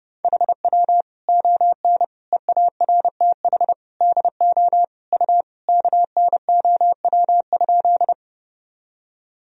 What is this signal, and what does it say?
Morse code, heard as a keyed tone, about 30 wpm, 700 Hz. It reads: HW ON EARTH DO U KNOW?
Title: Sentences from Top 200 Words and Abbreviations 30wpm